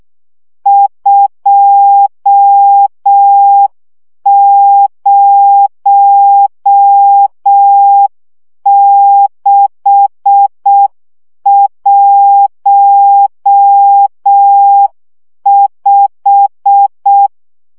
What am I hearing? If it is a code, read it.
Morse code, heard as a keyed tone, about 6 wpm, 800 Hz. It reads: 20615
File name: morze vizsga 2011-3 5